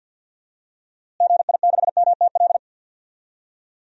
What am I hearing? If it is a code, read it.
Morse code, heard as a keyed tone, about 53 wpm, 700 Hz. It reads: GI6MTL
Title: DXCC-100 Call Signs - 6 Characters - Repeated 3x - 50wpm